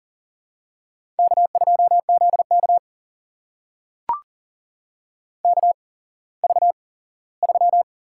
Morse code, heard as a keyed tone, about 37 wpm, spaced as for 19 wpm, 700 Hz. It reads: K2ZK KV3
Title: USA Call Signs - 4 Characters - 8x Character Spacing - 40wpm